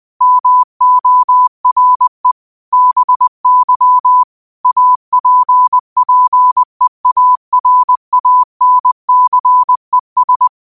Morse code, heard as a keyed tone, about 20 wpm, 1000 Hz. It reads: MORE BY APPEARANCES
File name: qotdfiles / 20wpm